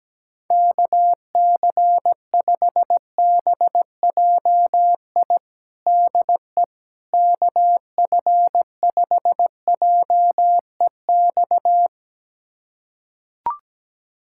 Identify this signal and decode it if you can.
Morse code, heard as a keyed tone, about 17 wpm, 700 Hz. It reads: KC5BJI DE KF5JEX
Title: USA Callsign DE Callsign - 6 Characters - 17wpm